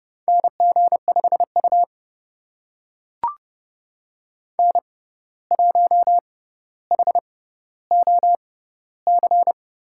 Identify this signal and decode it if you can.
Morse code, heard as a keyed tone, about 31 wpm, spaced as for 14 wpm, 700 Hz. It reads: NG5U N1HOC
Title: USA Call Signs - 6x Character Spacing - 30wpm